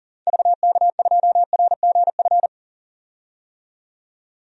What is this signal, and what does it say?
Morse code, heard as a keyed tone, about 40 wpm, 700 Hz. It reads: VK2RGF
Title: DXCC-100 Call Signs - 6 Characters - 40wpm